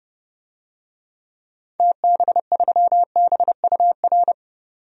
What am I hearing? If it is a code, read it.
Morse code, heard as a keyed tone, about 30 wpm, 700 Hz. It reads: TB3BUR
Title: DXCC-100 Call Signs - 6 Characters - 30wpm